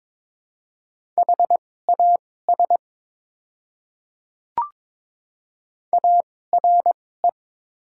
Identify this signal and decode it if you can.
Morse code, heard as a keyed tone, about 22 wpm, spaced as for 16 wpm, 700 Hz. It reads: HAS ARE
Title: 2x Character Spacing - Top 100 Words - 22wpm